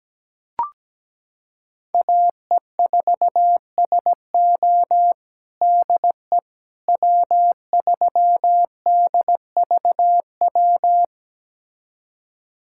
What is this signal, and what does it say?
Morse code, heard as a keyed tone, about 17 wpm, 700 Hz. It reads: AE4SO DE W3DVW
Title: USA Callsign DE Callsign - 5 Characters - 17wpm